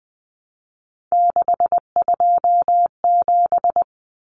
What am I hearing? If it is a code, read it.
Morse code, heard as a keyed tone, about 20 wpm, 700 Hz. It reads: 627
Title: Common Cut Numbers - 3-Digit 20wpm